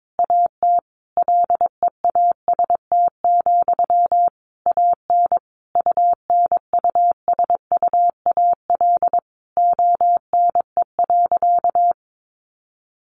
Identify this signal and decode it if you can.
Morse code, heard as a keyed tone, about 22 wpm, 700 Hz. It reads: AT LEAST, AN UNUSUAL ONE.
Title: Sentences from Top 1400 Words 22wpm